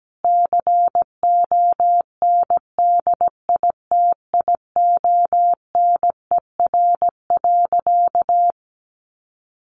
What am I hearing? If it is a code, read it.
Morse code, heard as a keyed tone, about 17 wpm, 700 Hz. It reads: CONDITIONER.
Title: Additional Easy Phrases 17wpm